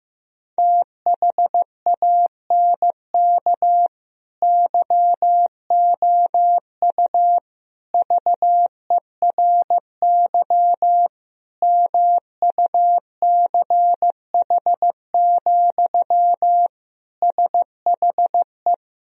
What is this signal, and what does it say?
Morse code, heard as a keyed tone, about 15 wpm, 700 Hz. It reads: THANK YOU VERY MUCH, SHE
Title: Sentences from Top 1000 Words - Review Part 4 - 15wpm